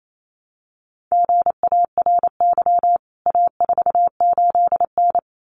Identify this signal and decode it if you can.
Morse code, heard as a keyed tone, about 28 wpm, 700 Hz. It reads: GARY A48N
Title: CWT Exchange 28wpm